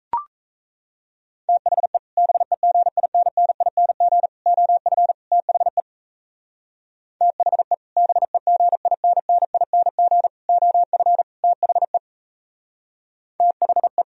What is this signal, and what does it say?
Morse code, heard as a keyed tone, about 38 wpm, 700 Hz. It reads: THE BEGINNING OF THE THE BEGINNING OF THE THE